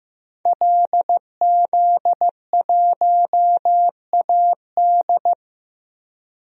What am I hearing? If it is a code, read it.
Morse code, heard as a keyed tone, about 15 wpm, 700 Hz. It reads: LZ1AD
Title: DXCC-100 Call Signs - 5 Characters - Repeated 2x - 15wpm